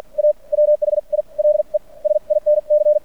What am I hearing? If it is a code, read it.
Morse code, heard as a keyed tone, about 50 wpm, spaced as for 39 wpm, 600 Hz. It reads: A WHILE SINC
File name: archived / CRSnet